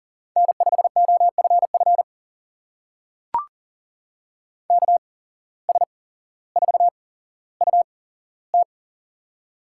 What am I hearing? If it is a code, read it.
Morse code, heard as a keyed tone, about 39 wpm, spaced as for 17 wpm, 700 Hz. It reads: N5OFF KS4UT
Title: USA Call Signs - 5 Characters - 8x Character Spacing - 40wpm